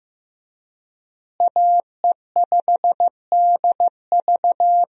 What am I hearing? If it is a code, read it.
Morse code, heard as a keyed tone, about 14 wpm, 700 Hz. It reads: AE5DV
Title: USA Call Signs - 5 Characters - Repeated 2x - 15wpm